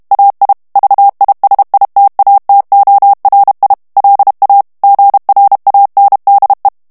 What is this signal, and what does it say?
Morse code, heard as a keyed tone, about 32 wpm, 800 Hz. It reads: AI VISITATORI LA GRANDE
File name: ITALIEN texte clair 1 vitesse 30 mots minute